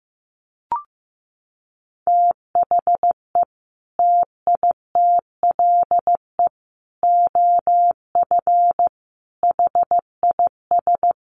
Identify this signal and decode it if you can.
Morse code, heard as a keyed tone, about 15 wpm, 700 Hz. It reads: THE TITLE OF HIS
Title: Sentences from Top 1200 Words 15wpm